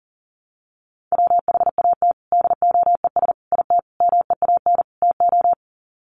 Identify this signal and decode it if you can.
Morse code, heard as a keyed tone, about 40 wpm, 700 Hz. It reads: WHAT DOES IT MEAN TO
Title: Sets of 5 Words Encore2 40wpm